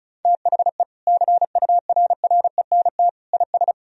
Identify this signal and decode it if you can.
Morse code, heard as a keyed tone, about 36 wpm, 700 Hz. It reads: THE CURRENT IS
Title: Sentences from Top 1000 Words - Review Part 4 - 35wpm